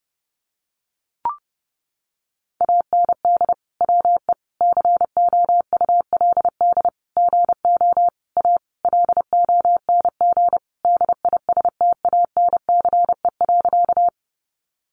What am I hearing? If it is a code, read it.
Morse code, heard as a keyed tone, about 30 wpm, 700 Hz. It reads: AND WE COULD GO A LONG DISTANCE.
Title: Sentences from Top 1200 Words 30wpm